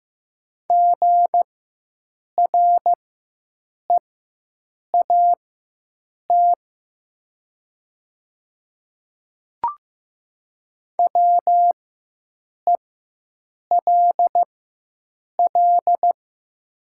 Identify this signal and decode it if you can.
Morse code, heard as a keyed tone, about 15 wpm, spaced as for 7 wpm, 700 Hz. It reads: GREAT WELL
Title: 4x Character Spacing - Top 300 Words - 15wpm